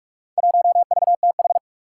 Morse code, heard as a keyed tone, about 47 wpm, 700 Hz. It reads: 1VTH
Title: Four Challenging Letter-Number Combinations 45wpm